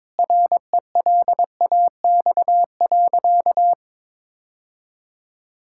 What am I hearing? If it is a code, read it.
Morse code, heard as a keyed tone, about 22 wpm, 700 Hz. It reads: RELAX.
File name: practice / 22wpm / Sentences with 8-Word Phrases - 22wpm